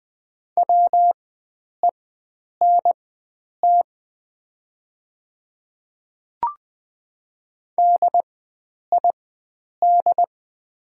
Morse code, heard as a keyed tone, about 20 wpm, spaced as for 9 wpm, 700 Hz. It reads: WENT DID
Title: 4x Character Spacing - Top 300 Words - 20wpm